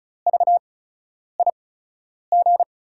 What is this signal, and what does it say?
Morse code, heard as a keyed tone, about 36 wpm, spaced as for 10 wpm, 700 Hz. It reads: VIG
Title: USA Call Signs - 8x Character Spacing - 35wpm